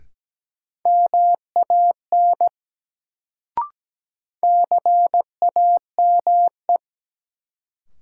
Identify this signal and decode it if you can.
Morse code, heard as a keyed tone, about 17 wpm, 700 Hz. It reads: MAN CAME
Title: Sentences from Top 200 Words - New Word Drill - 17wpm